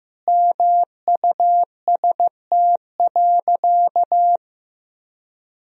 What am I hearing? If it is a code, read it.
Morse code, heard as a keyed tone, about 16 wpm, 700 Hz. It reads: MUST.
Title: 4x Word-Spacing - Sentences from Top 200 Words - 15wpm